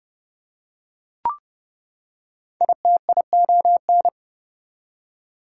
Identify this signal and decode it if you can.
Morse code, heard as a keyed tone, about 31 wpm, 700 Hz. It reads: ITION
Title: Prefixes and Suffixes 30wpm